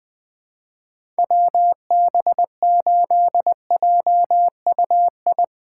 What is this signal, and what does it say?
Morse code, heard as a keyed tone, about 21 wpm, 700 Hz. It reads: WB8JUI
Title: K1USN SST QSO Practice - 20wpm